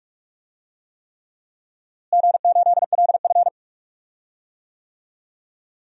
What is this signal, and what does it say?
Morse code, heard as a keyed tone, about 48 wpm, 700 Hz. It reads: G8LF
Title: DXCC-100 Call Signs - 4 Characters - Repeated 3x - 45wpm